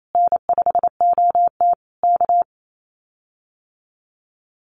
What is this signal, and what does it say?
Morse code, heard as a keyed tone, about 28 wpm, 700 Hz. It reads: N5OT K
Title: Common QSO Phrases Encore 28wpm